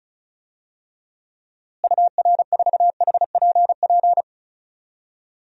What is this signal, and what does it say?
Morse code, heard as a keyed tone, about 36 wpm, 700 Hz. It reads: UR4HPP
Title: DXCC-100 Call Signs - 6 Characters - Repeated 3x - 35wpm